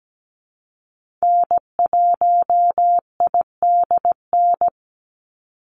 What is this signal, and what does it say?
Morse code, heard as a keyed tone, about 17 wpm, 700 Hz. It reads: N1IDN
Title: NA SOTA Activator Callsigns 17wpm